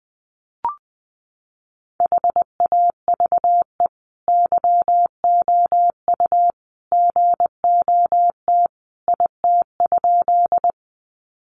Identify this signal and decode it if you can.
Morse code, heard as a keyed tone, about 20 wpm, 700 Hz. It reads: HAVE YOU GOT IT?
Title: Sentences from Top 300 Words 20wpm